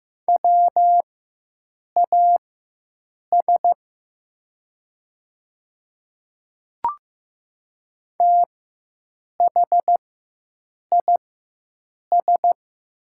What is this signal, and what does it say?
Morse code, heard as a keyed tone, about 15 wpm, spaced as for 7 wpm, 700 Hz. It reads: WAS THIS
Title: 4x Character Spacing - Top 100 Words - 15wpm